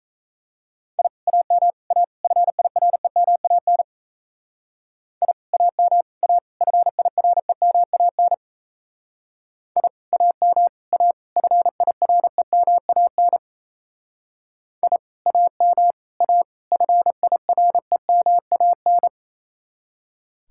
Morse code, heard as a keyed tone, about 36 wpm, 700 Hz. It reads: I AM A FIREMAN I AM A FIREMAN I AM A FIREMAN I AM A FIREMAN